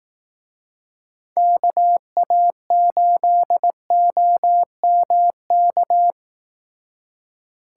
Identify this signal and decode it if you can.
Morse code, heard as a keyed tone, about 18 wpm, 700 Hz. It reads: KA8OMK
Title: Speed Racing - USA 2x3 Call Signs 15wpm